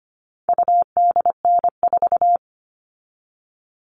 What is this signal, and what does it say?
Morse code, heard as a keyed tone, about 25 wpm, 700 Hz. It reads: UDN4
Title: All Cut Numbers - 4-Digit 25wpm